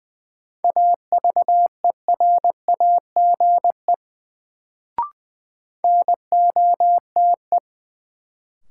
Morse code, heard as a keyed tone, about 20 wpm, 700 Hz. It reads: AVERAGE NOTE
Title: Sets of 5 Words - New Word Drill - 20wpm